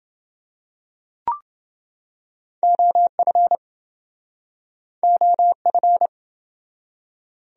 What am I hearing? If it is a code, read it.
Morse code, heard as a keyed tone, about 30 wpm, 700 Hz. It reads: OF OF